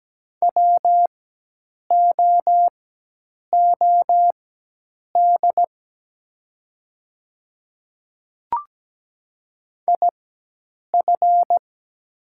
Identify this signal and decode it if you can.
Morse code, heard as a keyed tone, about 17 wpm, spaced as for 8 wpm, 700 Hz. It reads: WOOD IF
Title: 4x Character Spacing - Top 300 Words - 17wpm